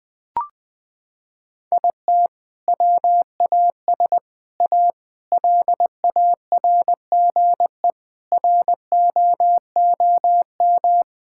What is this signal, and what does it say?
Morse code, heard as a keyed tone, about 20 wpm, 700 Hz. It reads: IT WAS A LARGE ROOM
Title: Sentences from Top 1600 Words 20wpm